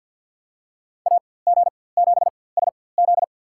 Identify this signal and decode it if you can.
Morse code, heard as a keyed tone, about 47 wpm, spaced as for 23 wpm, 700 Hz. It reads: AG7SZ
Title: USA Call Signs - 5 Characters - 4x Character Spacing - 50wpm